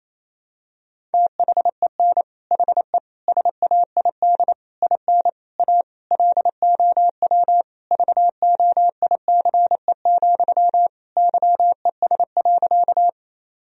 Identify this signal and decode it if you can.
Morse code, heard as a keyed tone, about 28 wpm, 700 Hz. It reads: THEN HE SAID IN A LOW VOICE, YES.
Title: Sentences from Top 1000 Words - Review Part 1 - 28wpm